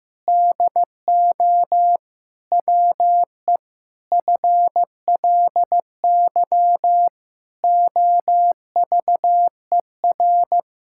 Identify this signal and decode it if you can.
Morse code, heard as a keyed tone, about 16 wpm, 700 Hz. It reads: DO WE FLY OVER
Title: Pangram Challenge v2 - 15wpm